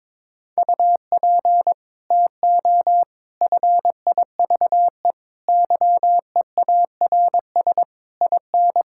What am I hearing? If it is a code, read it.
Morse code, heard as a keyed tone, about 21 wpm, 700 Hz. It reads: UP TO FIVE YEARS IN
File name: practice / 22wpm / Sets of 6 Words - 22wpm